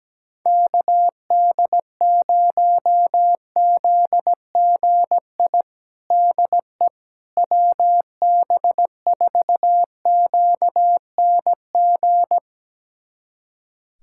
Morse code, heard as a keyed tone, about 17 wpm, 700 Hz. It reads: KD0ZGI DE WB4QNG
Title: USA Callsign DE Callsign - 6 Characters - 17wpm